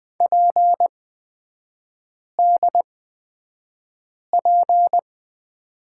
Morse code, heard as a keyed tone, about 20 wpm, spaced as for 5 wpm, 700 Hz. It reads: PDP
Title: 250 Lesson 40e - BK - Three Characters 20wpm